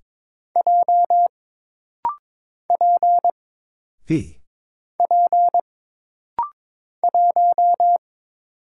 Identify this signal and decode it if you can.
Morse code, heard as a keyed tone, about 22 wpm, 700 Hz. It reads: J P P 1
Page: This rapid-fire timing is geared to push your ICR (Instant Character Recognition) skills.